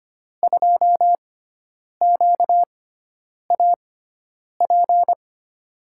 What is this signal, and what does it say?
Morse code, heard as a keyed tone, about 25 wpm, spaced as for 9 wpm, 700 Hz. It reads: 2QAP
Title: USA Call Signs - 6 Characters - 6x Character Spacing - 25wpm